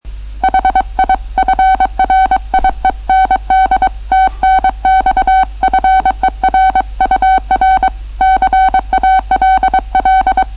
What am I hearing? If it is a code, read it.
Morse code, heard as a keyed tone, about 23 wpm, 760 Hz. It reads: HI FRIEND TNX FER UR CALL